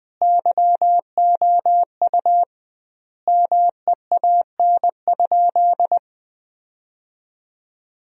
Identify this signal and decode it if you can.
Morse code, heard as a keyed tone, about 20 wpm, 700 Hz. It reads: YOU MEAN?
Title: 2x Word-Spacing - Sentences from Top 200 Words - 20wpm